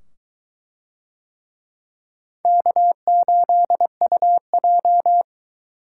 Morse code, heard as a keyed tone, about 22 wpm, 700 Hz. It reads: K8UJ